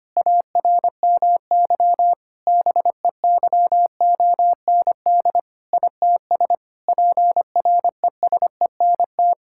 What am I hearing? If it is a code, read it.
Morse code, heard as a keyed tone, about 25 wpm, 700 Hz. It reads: ARMY BEYOND ITS PRESENT
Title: Sentences from Top 1000 Words - Review Part 6 - 25wpm